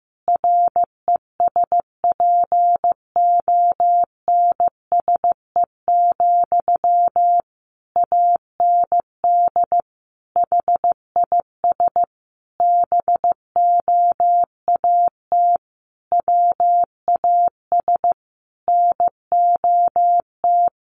Morse code, heard as a keyed tone, about 15 wpm, 700 Hz. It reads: RESPONSE, AND HIS BOAT WAS NOT
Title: Sentences from Top 1400 Words 15wpm